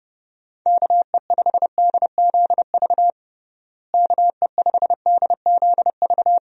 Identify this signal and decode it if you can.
Morse code, heard as a keyed tone, about 32 wpm, 700 Hz. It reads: KE5DZV KE5DZV
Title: USA Call Signs - 6 Characters - Repeated 2x - 30wpm